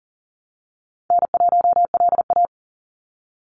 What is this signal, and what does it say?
Morse code, heard as a keyed tone, about 40 wpm, 700 Hz. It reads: N1LA
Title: USA Call Signs 40wpm